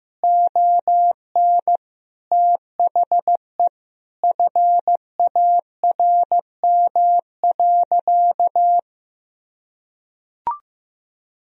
Morse code, heard as a keyed tone, about 16 wpm, 700 Hz. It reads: ON THE FARM.
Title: Sentences from Top 1000 Words - Review Part 2 - 15wpm